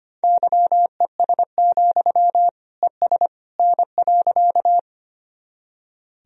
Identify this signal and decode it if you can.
Morse code, heard as a keyed tone, about 25 wpm, 700 Hz. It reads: YES, ES N.
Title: Sentences from Top 500 Words and Abbreviations 25wpm